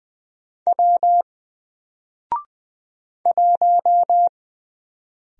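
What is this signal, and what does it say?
Morse code, heard as a keyed tone, about 20 wpm, 700 Hz. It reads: W 1
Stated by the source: Single letter and number Morse drill at 20 wpm, from N4LCD.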